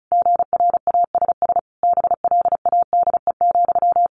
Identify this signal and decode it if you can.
Morse code, heard as a keyed tone, about 34 wpm, 700 Hz. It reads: GRASS BLADE,
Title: A Sound Of Thunder 35wpm